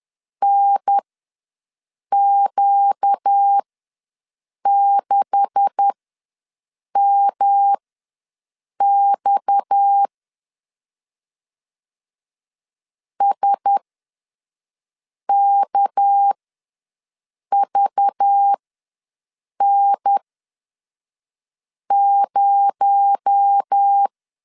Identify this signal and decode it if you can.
Morse code, heard as a keyed tone, about 10 wpm, spaced as for 5 wpm, 800 Hz. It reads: NQ6MX SKVN0
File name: LEZIONI / 9 286 ws 10wpm 4032